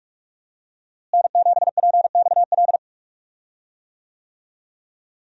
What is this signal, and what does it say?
Morse code, heard as a keyed tone, about 42 wpm, 700 Hz. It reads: N7PXL
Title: USA Call Signs - 5 Characters - Repeated 3x - 45wpm